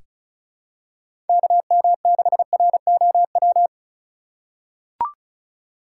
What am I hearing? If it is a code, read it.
Morse code, heard as a keyed tone, about 36 wpm, 700 Hz. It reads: KM6ROW
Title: USA Call Signs - 6 Characters - 4x Character Spacing - 35wpm